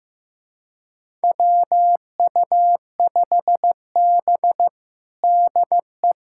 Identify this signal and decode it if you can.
Morse code, heard as a keyed tone, about 15 wpm, 700 Hz. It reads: WU5B DE
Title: USA Callsign DE Callsign - 4 Characters - 15wpm